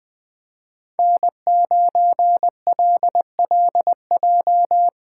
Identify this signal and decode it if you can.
Morse code, heard as a keyed tone, about 24 wpm, 700 Hz. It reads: N9LLJ
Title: Speed Racing - USA 1x3 Call Signs 20wpm